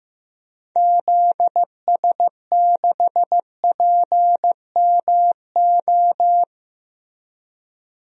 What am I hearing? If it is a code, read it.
Morse code, heard as a keyed tone, about 15 wpm, 700 Hz. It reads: ZS6PMO
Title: DXCC-100 Call Signs - 6 Characters - Repeated 2x - 15wpm